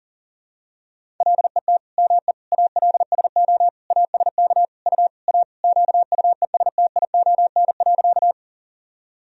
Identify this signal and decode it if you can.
Morse code, heard as a keyed tone, about 40 wpm, 700 Hz. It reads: LET ME ALSO ASK U A QUESTION.
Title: Sentences from Top 400 Words and Abbreviations 40wpm